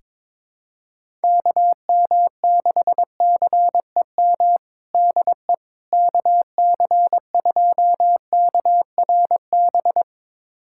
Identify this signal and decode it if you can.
Morse code, heard as a keyed tone, about 22 wpm, 700 Hz. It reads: KM6CEM DE KC2KRB
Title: USA Callsign DE Callsign - 6 Characters - 22wpm